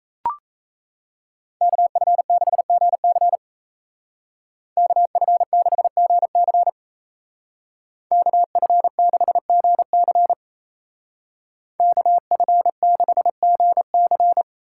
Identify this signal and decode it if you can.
Morse code, heard as a keyed tone, about 36 wpm, 700 Hz. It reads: KF6GC KF6GC KF6GC KF6GC